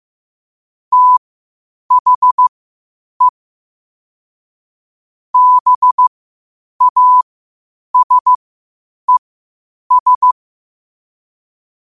Morse code, heard as a keyed tone, about 15 wpm, spaced as for 8 wpm, 1000 Hz. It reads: THE BASES
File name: qotdfiles / 8wpm